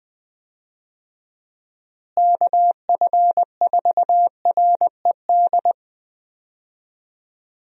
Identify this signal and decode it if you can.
Morse code, heard as a keyed tone, about 20 wpm, 700 Hz. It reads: KF4RED
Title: USA Call Signs - 6 Characters - Repeated 3x - 20wpm